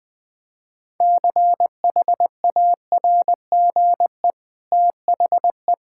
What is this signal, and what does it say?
Morse code, heard as a keyed tone, about 20 wpm, 700 Hz. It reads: CHARGE THE
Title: Easy Sentences with Hints - 20wpm